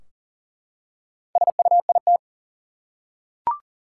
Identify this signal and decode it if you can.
Morse code, heard as a keyed tone, about 40 wpm, 700 Hz. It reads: SUIT
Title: Four Letter Words - 40wpm